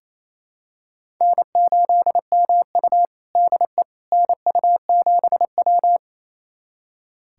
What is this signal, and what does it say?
Morse code, heard as a keyed tone, about 28 wpm, 700 Hz. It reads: N8MU DE NU7W
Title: USA Callsign DE Callsign - 4 Characters - 28wpm